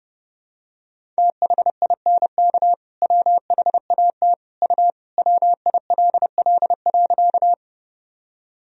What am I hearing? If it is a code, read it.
Morse code, heard as a keyed tone, about 32 wpm, 700 Hz. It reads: THINK WHAT U WILL.
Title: Sentences from Top 200 Words and Abbreviations 30wpm